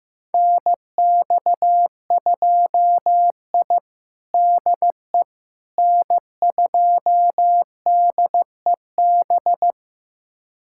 Morse code, heard as a keyed tone, about 16 wpm, 700 Hz. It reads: NX2I DE N2DEB
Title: USA Callsign DE Callsign 15wpm